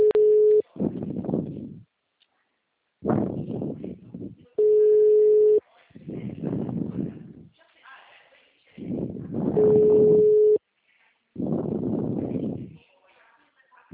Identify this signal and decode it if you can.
Ringback tone (Europe, 425 Hz), with speech.
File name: out- -2053- - - .96288